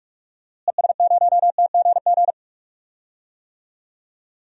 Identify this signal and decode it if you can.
Morse code, heard as a keyed tone, about 43 wpm, 700 Hz. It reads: ES0TGG
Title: DXCC-100 Call Signs - 6 Characters - Repeated 3x - 45wpm